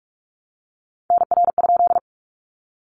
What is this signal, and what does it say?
Morse code, heard as a keyed tone, about 44 wpm, 700 Hz. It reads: NR?
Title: Common QSO Phrases 45wpm